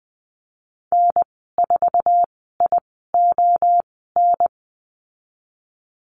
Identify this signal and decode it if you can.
Morse code, heard as a keyed tone, about 20 wpm, spaced as for 14 wpm, 700 Hz. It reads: N4ION
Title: 048 Lesson 9g - Number 4 - Call Signs 20wpm